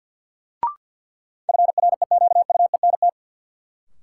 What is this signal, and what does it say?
Morse code, heard as a keyed tone, about 52 wpm, 700 Hz. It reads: FREQUENT
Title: Sentences from Top 1400 Words - New Word Drill Alt - 50wpm